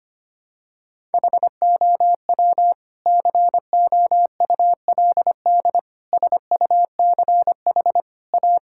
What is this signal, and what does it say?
Morse code, heard as a keyed tone, about 25 wpm, 700 Hz. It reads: HOW COULD SUCH A
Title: Sentences from Top 1000 Words - Review Part 6 - 25wpm